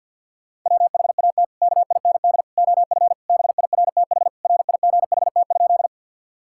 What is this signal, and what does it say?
Morse code, heard as a keyed tone, about 50 wpm, 700 Hz. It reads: WHAT KIND OF BIRTH RIGHT?
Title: Sentences from Top 1000 Words - Review Part 5 - 50wpm